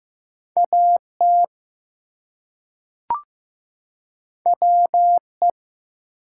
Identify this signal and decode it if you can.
Morse code, heard as a keyed tone, about 15 wpm, 700 Hz. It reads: AT WE
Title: Two Letter Words - 15wpm